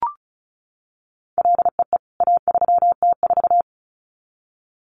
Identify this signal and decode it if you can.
Morse code, heard as a keyed tone, about 34 wpm, 700 Hz. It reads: LEE A3T4
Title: CWT Exchange 35wpm